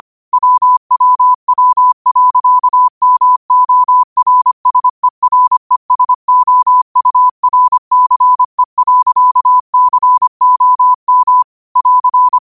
Morse code, heard as a keyed tone, about 25 wpm, 1000 Hz. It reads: WWW.MORSERESOURCE.COM <AR>
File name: qotdfiles / 25wpm